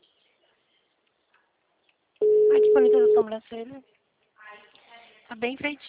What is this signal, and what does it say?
Ringback tone (Europe, 425 Hz), with speech.